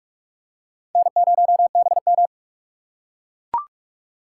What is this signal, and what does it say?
Morse code, heard as a keyed tone, about 41 wpm, 700 Hz. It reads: N0BM
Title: USA Call Signs - 4 Characters - 4x Character Spacing - 45wpm